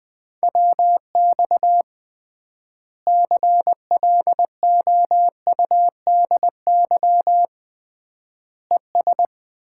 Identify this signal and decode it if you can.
Morse code, heard as a keyed tone, about 21 wpm, 700 Hz. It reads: WX CLOUDY ES
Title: Common QSO Phrases - 3x Word-Spacing - 20wpm